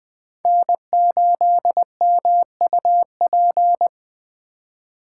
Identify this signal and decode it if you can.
Morse code, heard as a keyed tone, about 20 wpm, 700 Hz. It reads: N8MUP
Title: USA Call Signs - 5 Characters - 4x Character Spacing - 20wpm